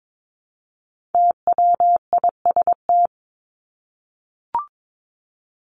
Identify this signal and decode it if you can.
Morse code, heard as a keyed tone, about 22 wpm, 700 Hz. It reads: TWIST
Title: Top 1751-2000 Words 22wpm